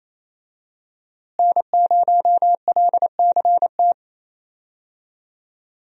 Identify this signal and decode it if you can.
Morse code, heard as a keyed tone, about 27 wpm, 700 Hz. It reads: N0LCT
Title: USA Call Signs - 5 Characters - Repeated 2x - 28wpm